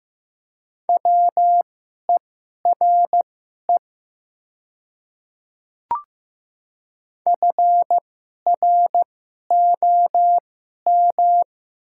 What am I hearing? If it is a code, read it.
Morse code, heard as a keyed tone, about 15 wpm, spaced as for 11 wpm, 700 Hz. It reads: WERE FROM
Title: 2x Character Spacing - Top 300 Words - 15wpm